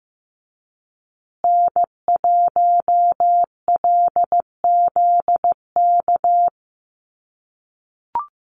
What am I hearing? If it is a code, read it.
Morse code, heard as a keyed tone, about 15 wpm, 700 Hz. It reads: N1LZK
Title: USA Call Signs Repeated 2x 15wpm